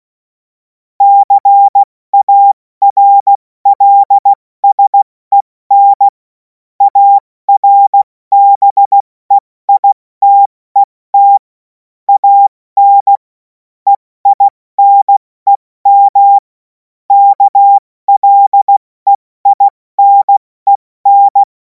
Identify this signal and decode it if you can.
Morse code, heard as a keyed tone, about 16 wpm, spaced as for 14 wpm, 800 Hz. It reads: CARLSEN ARBEITET AN EINEM KLEINEN
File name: Morsekurs / Witze / Witz 31. März 2026 70 F80